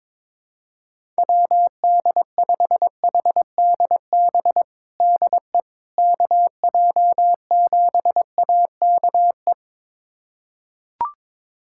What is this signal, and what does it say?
Morse code, heard as a keyed tone, about 22 wpm, 700 Hz. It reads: WD5HDB DE KJ7AKE
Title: USA Callsign DE Callsign - 6 Characters - 22wpm